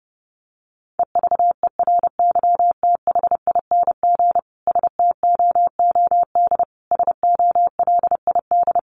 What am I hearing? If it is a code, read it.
Morse code, heard as a keyed tone, about 30 wpm, 700 Hz. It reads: EVERYTHING STOOD SOLID
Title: Sentences from Top 1500 Words 30wpm